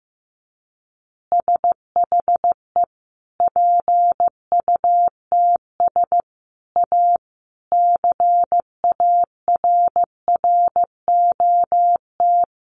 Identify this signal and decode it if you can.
Morse code, heard as a keyed tone, about 14 wpm, 700 Hz. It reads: SHE PUTS A CARROT
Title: Easy Phrases 15wpm